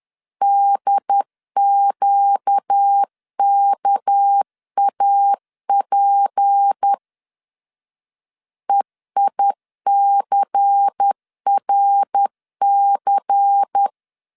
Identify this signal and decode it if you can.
Morse code, heard as a keyed tone, about 10 wpm, 800 Hz. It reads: DQKAP EICRC
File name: LEZIONI / 6 PKX0 / 6 PKX0 10wpm 2707